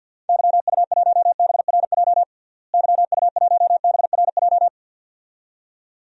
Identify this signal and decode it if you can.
Morse code, heard as a keyed tone, about 52 wpm, 700 Hz. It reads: YU1BRJ YU1BRJ
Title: DXCC-100 Call Signs - 6 Characters - Repeated 2x - 50wpm